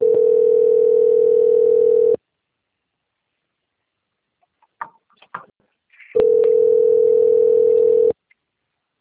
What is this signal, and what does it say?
Ringback tone (North America), 440 + 480 Hz.
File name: out- - - - - .42652